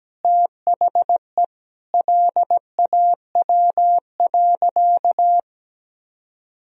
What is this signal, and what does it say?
Morse code, heard as a keyed tone, about 17 wpm, 700 Hz. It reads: THE LAW.
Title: Sentences from Top 1000 Words - Review Part 4 - 17wpm